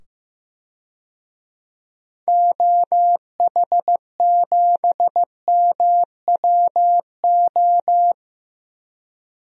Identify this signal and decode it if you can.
Morse code, heard as a keyed tone, about 15 wpm, 700 Hz. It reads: OH7MWO
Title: DXCC-100 Call Signs - 6 Characters - 15wpm